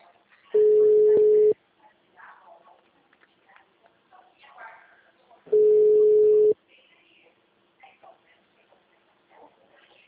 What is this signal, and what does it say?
Ringback tone (Europe, 425 Hz), with speech.
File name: out- -2005- - - .32718